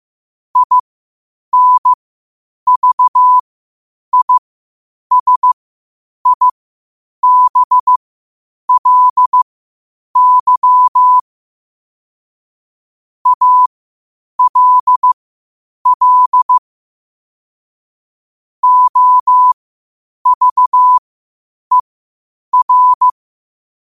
Morse code, heard as a keyed tone, about 15 wpm, spaced as for 8 wpm, 1000 Hz. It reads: INVISIBLY ALL OVER
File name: qotdfiles / 8wpm